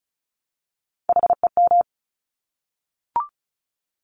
Morse code, heard as a keyed tone, about 36 wpm, 700 Hz. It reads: HEM
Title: Top Three Letter Combinations in Words 35wpm